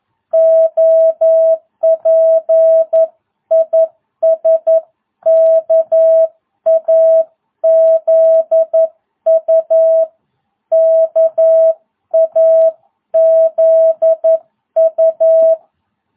Morse code, heard as a keyed tone, about 11 wpm, 660 Hz. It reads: OP IS KAZU KAZU